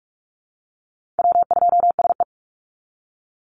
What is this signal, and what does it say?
Morse code, heard as a keyed tone, about 44 wpm, 700 Hz. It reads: W2SE
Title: NA SOTA Chaser Callsigns 45wpm